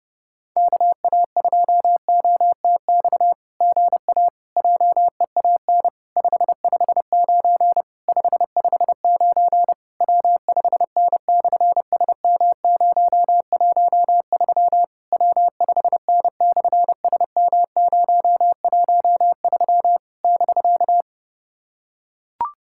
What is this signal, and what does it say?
Morse code, heard as a keyed tone, about 29 wpm, 700 Hz. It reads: KA2OTX GA JEAN 559 559 W5N/SM013 W5N/SM013 <BK>
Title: SOTA QSO Practice 30wpm